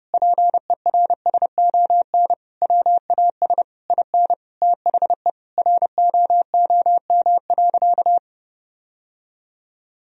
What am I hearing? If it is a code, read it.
Morse code, heard as a keyed tone, about 32 wpm, 700 Hz. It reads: PERSON WAS IN THE ROOM.
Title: Sentences from Top 1000 Words - Review Part 4 - 30wpm